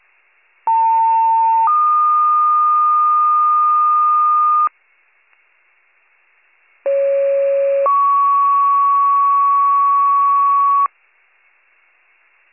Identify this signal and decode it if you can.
Two-tone page, heard as a one-second tone, then a three-second tone, about 900 then 1220 Hz, about 550 then 1120 Hz. It is heard from 0.7 to 10.9 s.